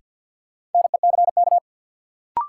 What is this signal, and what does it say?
Morse code, heard as a keyed tone, about 49 wpm, 700 Hz. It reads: NECK
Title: Sets of 5 Words Encore - New Word Drill - 50wpm